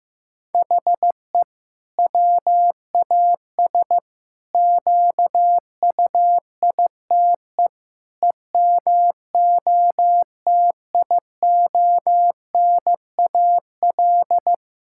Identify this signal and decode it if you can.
Morse code, heard as a keyed tone, about 15 wpm, 700 Hz. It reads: HE WAS QUITE EMOTIONAL
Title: Sentences from Top 1000 Words - Review Part 5 - 15wpm